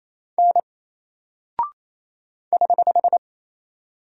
Morse code, heard as a keyed tone, about 28 wpm, 700 Hz. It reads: N <HH>
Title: Single Character (Letter:Number:Punctuation:Prosign) 28wpm